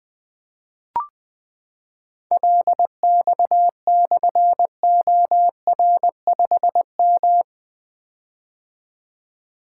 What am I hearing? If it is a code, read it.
Morse code, heard as a keyed tone, about 20 wpm, 700 Hz. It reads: LX/OR5M
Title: Difficult Call Signs 20wpm